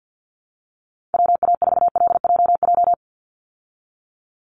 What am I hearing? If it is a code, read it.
Morse code, heard as a keyed tone, about 50 wpm, 700 Hz. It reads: WA4LJJ
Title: Call Signs Repeated 3x - 50wpm